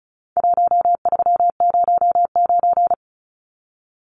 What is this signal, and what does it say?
Morse code, heard as a keyed tone, about 34 wpm, 700 Hz. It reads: 1309
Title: Numbers - 4-Digit 35wpm